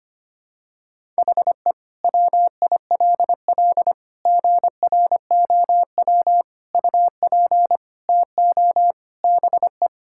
Morse code, heard as a keyed tone, about 25 wpm, 700 Hz. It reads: HE WILL GROW UP TO BE
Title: Sentences from Top 1000 Words - Review Part 6 - 25wpm